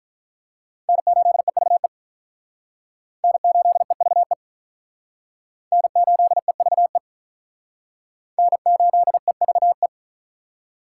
Morse code, heard as a keyed tone, about 43 wpm, 700 Hz. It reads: N8EVE N8EVE N8EVE N8EVE
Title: Speed Racing - USA 1x3 Call Signs 35wpm